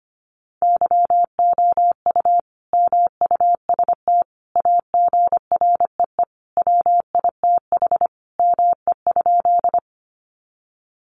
Morse code, heard as a keyed tone, about 25 wpm, 700 Hz. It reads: YOU MUST AGREE WITH ME?
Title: Sentences from Top 1100 Words 25wpm